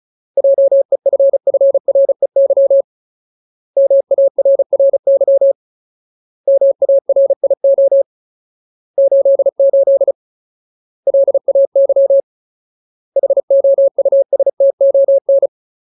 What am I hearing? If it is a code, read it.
Morse code, heard as a keyed tone, about 34 wpm, 550 Hz. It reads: JEFFREY MARRY MARIO 88 LAY HOUSTON
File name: Ultimate Word List / 35WPM Ultimate